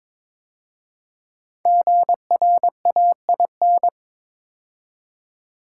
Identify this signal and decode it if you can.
Morse code, heard as a keyed tone, about 22 wpm, 700 Hz. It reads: GRAIN
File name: 5 Letter Words - First 3 Letters Given - 22wpm